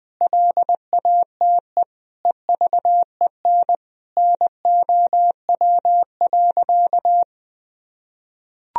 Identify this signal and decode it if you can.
Morse code, heard as a keyed tone, about 20 wpm, 700 Hz. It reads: LATE EVEN NOW.
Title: Sentences from Top 1000 Words - Review Part 1 - 20wpm